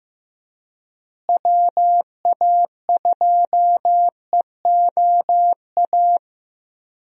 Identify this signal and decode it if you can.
Morse code, heard as a keyed tone, about 16 wpm, 700 Hz. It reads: WA2EOA
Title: USA Call Signs - 6 Characters - Repeated 3x - 15wpm